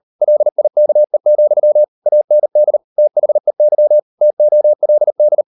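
Morse code, heard as a keyed tone, about 38 wpm, 600 Hz. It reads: LIKE, AND THEY TOLD
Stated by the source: Morse code practice at 38 WPM, sending Ken Follett's novel Never in English, from CW med Gnister.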